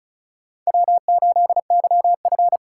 Morse code, heard as a keyed tone, about 35 wpm, 700 Hz. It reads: W8YF
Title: USA Call Signs - 4 Characters - 4x Character Spacing - 35wpm